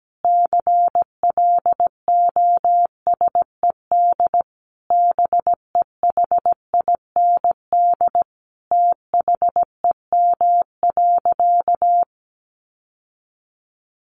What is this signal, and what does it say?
Morse code, heard as a keyed tone, about 17 wpm, 700 Hz. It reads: CLOSED BEHIND THEM.
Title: Sentences from Top 1400 Words 17wpm